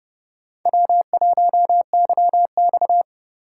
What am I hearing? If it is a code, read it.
Morse code, heard as a keyed tone, about 27 wpm, 700 Hz. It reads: W1YX